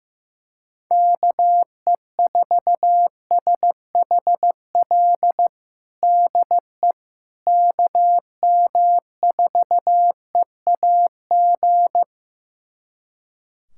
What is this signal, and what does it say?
Morse code, heard as a keyed tone, about 15 wpm, 700 Hz. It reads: KE4SHL DE KM4EAG
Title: USA Callsign DE Callsign - 6 Characters - 15wpm